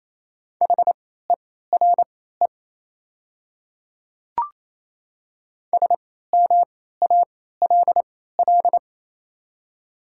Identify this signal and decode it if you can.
Morse code, heard as a keyed tone, about 28 wpm, spaced as for 16 wpm, 700 Hz. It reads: HERE SMALL
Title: 3x Character Spacing - Top 200 Words - 28wpm